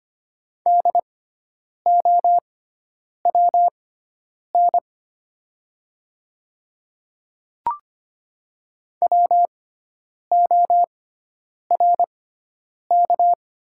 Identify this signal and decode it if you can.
Morse code, heard as a keyed tone, about 25 wpm, spaced as for 9 wpm, 700 Hz. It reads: DOWN WORK
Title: 6x Character Spacing - Top 200 Words - 25wpm